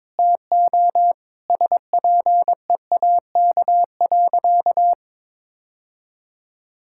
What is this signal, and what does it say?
Morse code, heard as a keyed tone, about 22 wpm, 700 Hz. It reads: TO SPEAK.
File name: practice / 22wpm / Sentences with 3-Word Phrases - 22wpm